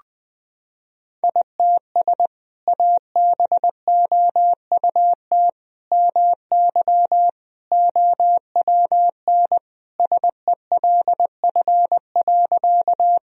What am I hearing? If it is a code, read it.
Morse code, heard as a keyed tone, about 20 wpm, 700 Hz. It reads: ITS ABOUT MY OWN SELF.
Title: Sentences from Top 1000 Words - Review Part 1 - 20wpm